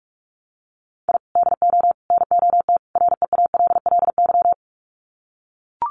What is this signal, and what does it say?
Morse code, heard as a keyed tone, about 44 wpm, 700 Hz. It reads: I DO NOT REALLY
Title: Sets of 4 Words 45wpm